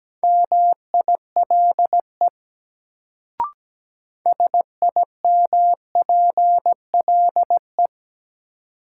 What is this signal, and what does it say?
Morse code, heard as a keyed tone, about 17 wpm, 700 Hz. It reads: MILE SIMPLE
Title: Sentences from Top 400 Words - New Word Drill - 17wpm